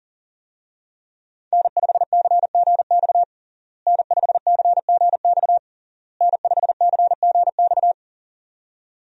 Morse code, heard as a keyed tone, about 40 wpm, 700 Hz. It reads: N5CGX N5CGX N5CGX
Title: USA Call Signs - 5 Characters - Repeated 3x - 40wpm